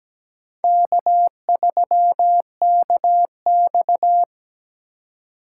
Morse code, heard as a keyed tone, about 20 wpm, 700 Hz. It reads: K3KX
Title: Speed Racing - USA 1x2 Call Signs 17wpm